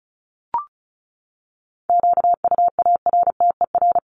Morse code, heard as a keyed tone, about 34 wpm, 700 Hz. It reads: QUARTER
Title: Top 1001-1250 Words 35wpm